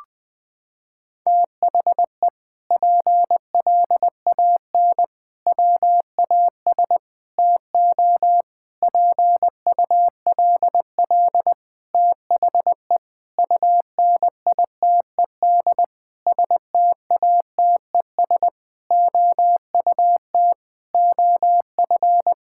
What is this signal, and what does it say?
Morse code, heard as a keyed tone, about 20 wpm, 700 Hz. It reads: THE PLAN WAS TO PULL THE UNITED STATES OUT OF
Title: Sentences with 8-Word Phrases - 20wpm